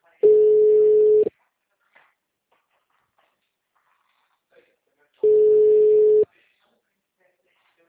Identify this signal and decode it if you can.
Ringback tone (Europe, 425 Hz).